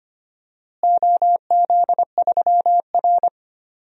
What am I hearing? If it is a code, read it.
Morse code, heard as a keyed tone, about 25 wpm, 700 Hz. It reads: OZ3R
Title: DXCC-100 Call Signs - 4 Characters - Repeated 2x - 25wpm